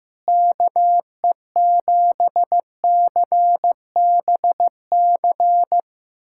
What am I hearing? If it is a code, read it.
Morse code, heard as a keyed tone, about 16 wpm, 700 Hz. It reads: KE7CBC
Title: USA Callsign DE Callsign 15wpm